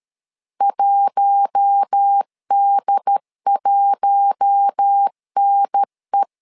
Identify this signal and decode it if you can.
Morse code, heard as a keyed tone, about 13 wpm, 800 Hz. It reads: 1D1NE
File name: LEZIONI / 4 Q1 12wpm 2028